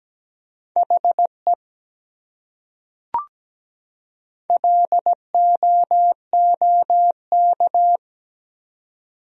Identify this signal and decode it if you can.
Morse code, heard as a keyed tone, about 17 wpm, 700 Hz. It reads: HE LOOK
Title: Top 100 Words 17wpm